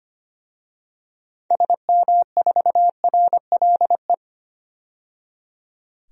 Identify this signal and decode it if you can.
Morse code, heard as a keyed tone, about 25 wpm, 700 Hz. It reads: SM4RLE
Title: DXCC-100 Call Signs - 6 Characters - 25wpm